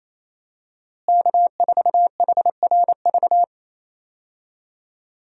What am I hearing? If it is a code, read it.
Morse code, heard as a keyed tone, about 28 wpm, 700 Hz. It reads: K4HRV
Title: USA Call Signs - 5 Characters - Repeated 2x - 28wpm